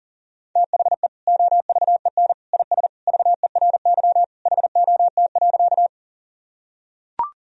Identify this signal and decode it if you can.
Morse code, heard as a keyed tone, about 40 wpm, 700 Hz. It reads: THE OVEN IS VERY HOT.
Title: Sentences from Top 1000 Words - Review Part 4 - 40wpm